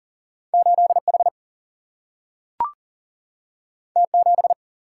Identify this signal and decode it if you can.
Morse code, heard as a keyed tone, about 39 wpm, 700 Hz. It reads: 8 H T 7
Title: Single-Letter Single-Number 40wpm